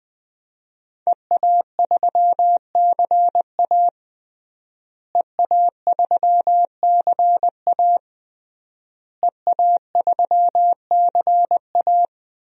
Morse code, heard as a keyed tone, about 20 wpm, 700 Hz. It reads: EA3CA EA3CA EA3CA
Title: DXCC-100 Call Signs - 5 Characters - Repeated 3x - 20wpm